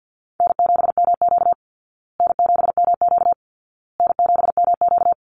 Morse code, heard as a keyed tone, about 48 wpm, 700 Hz. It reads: N7MQ N7MQ N7MQ
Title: Easy Call Signs Repeated 3x - 50wpm